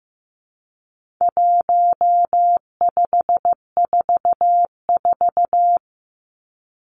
Morse code, heard as a keyed tone, about 15 wpm, 700 Hz. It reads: 1544
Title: Numbers - 4-Digit 15wpm